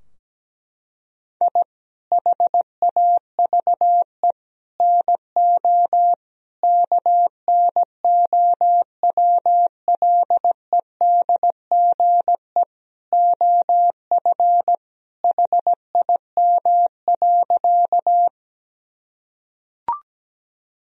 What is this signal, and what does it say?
Morse code, heard as a keyed tone, about 17 wpm, 700 Hz. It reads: I HAVE NO KNOWLEDGE OF HIM.
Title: Sentences from Top 1000 Words - Review Part 3 - 17wpm